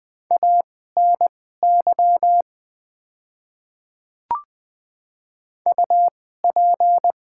Morse code, heard as a keyed tone, about 20 wpm, spaced as for 14 wpm, 700 Hz. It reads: ANY UP
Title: 2x Character Spacing - Top 300 Words - 20wpm